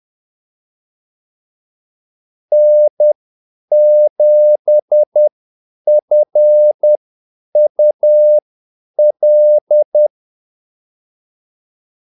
Morse code, heard as a keyed tone, about 10 wpm, spaced as for 8 wpm, 600 Hz. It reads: N7FUL
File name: Callsigns-10-8wpm